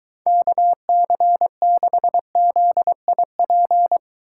Morse code, heard as a keyed tone, about 22 wpm, 700 Hz. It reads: KC6ZIP